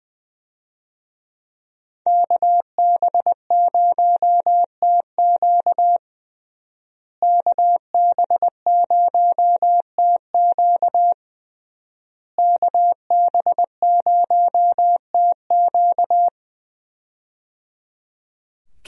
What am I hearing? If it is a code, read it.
Morse code, heard as a keyed tone, about 20 wpm, 700 Hz. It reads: KB0TQ KB0TQ KB0TQ
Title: USA Call Signs - 5 Characters - Repeated 3x - 20wpm